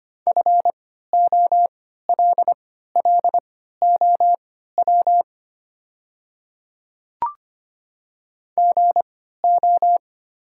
Morse code, heard as a keyed tone, about 25 wpm, spaced as for 14 wpm, 700 Hz. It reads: FOLLOW GO
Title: 3x Character Spacing - Top 200 Words - 25wpm